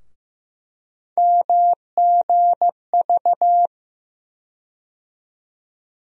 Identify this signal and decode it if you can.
Morse code, heard as a keyed tone, about 15 wpm, 700 Hz. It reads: MGV
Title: USA Call Sign Postfixes - 3 Letters 15wpm